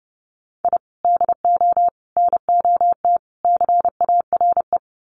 Morse code, heard as a keyed tone, about 30 wpm, 700 Hz. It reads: I DO NOT CARE
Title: Sets of 4 Words 30wpm